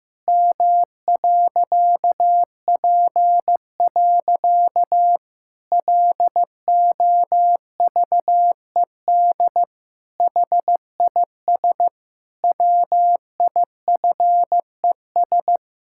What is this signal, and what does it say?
Morse code, heard as a keyed tone, about 15 wpm, 700 Hz. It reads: M.P. LOVED HIS WIFES
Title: Pangram Challenge v2 - 15wpm